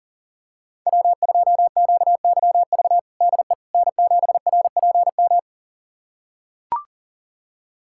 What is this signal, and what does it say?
Morse code, heard as a keyed tone, about 40 wpm, 700 Hz. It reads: W2QYV DE N7RPM
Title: USA Callsign DE Callsign - 5 Characters - 40wpm